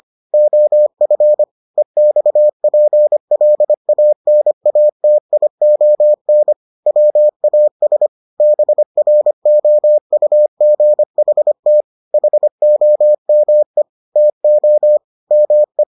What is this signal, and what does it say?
Morse code, heard as a keyed tone, about 25 wpm, 600 Hz. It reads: OF EXPLANATION WAS BROUGHT HOME TO ME